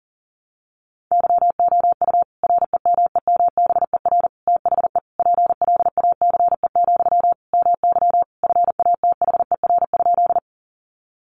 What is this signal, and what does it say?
Morse code, heard as a keyed tone, about 40 wpm, 700 Hz. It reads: YOU REMEMBER THE PLACE, MY FATHER?
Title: Sentences from Top 400 Words 40wpm